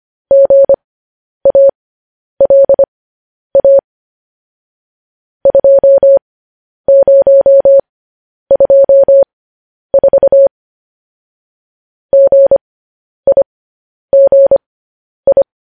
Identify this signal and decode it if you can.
Morse code, heard as a keyed tone, about 25 wpm, spaced as for 10 wpm, 550 Hz. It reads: GALA 2024 GIGI